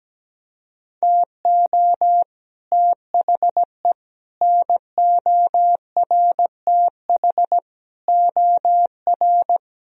Morse code, heard as a keyed tone, about 17 wpm, 700 Hz. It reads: TO THE NORTH OR
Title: Sentences from Top 1000 Words - Review Part 2 - 17wpm